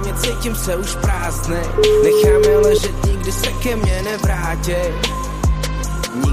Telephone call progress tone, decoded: ringback tone

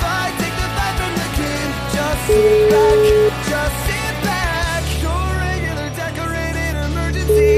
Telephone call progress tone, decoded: ringback tone

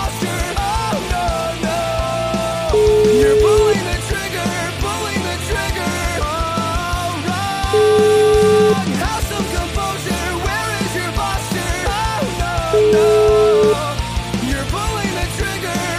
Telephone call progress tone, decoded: ringback tone

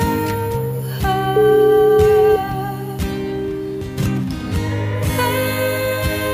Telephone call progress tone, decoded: ringback tone

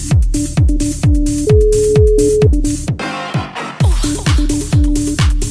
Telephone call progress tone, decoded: ringback tone

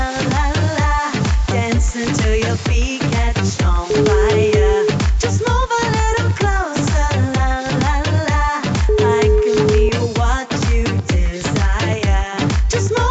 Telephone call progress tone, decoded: ringback tone

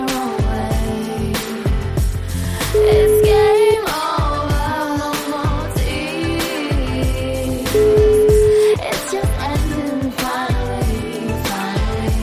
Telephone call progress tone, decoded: ringback tone

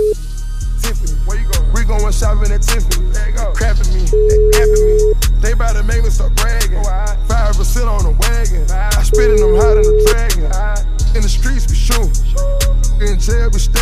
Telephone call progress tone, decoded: ringback tone

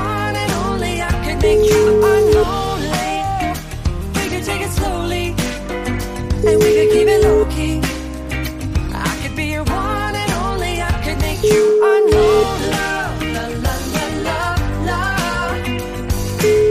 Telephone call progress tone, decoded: ringback tone

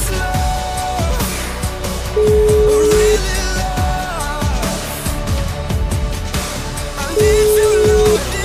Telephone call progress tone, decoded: ringback tone

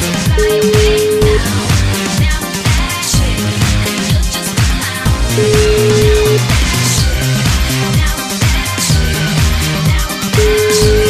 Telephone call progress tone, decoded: ringback tone

